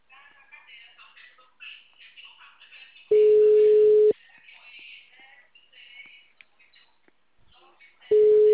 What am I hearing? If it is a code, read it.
Ringback tone (Europe, 425 Hz), with speech.